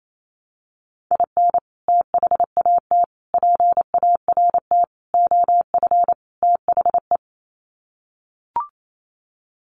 Morse code, heard as a keyed tone, about 28 wpm, 700 Hz. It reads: IN THAT PART OF THE
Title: Sets of 5 Words Encore2 28wpm